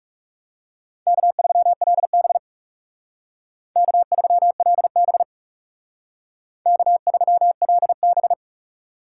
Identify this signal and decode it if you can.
Morse code, heard as a keyed tone, about 36 wpm, 700 Hz. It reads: K3LB K3LB K3LB
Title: Speed Racing - USA 1x2 Call Signs 30wpm